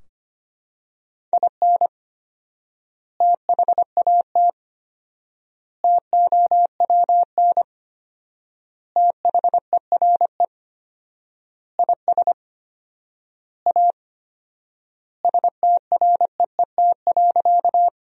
Morse code, heard as a keyed tone, about 25 wpm, 700 Hz. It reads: IN THAT TOWN THERE IS A STREET.
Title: 4x Word-Spacing - Sentences from Top 500 Words - 25wpm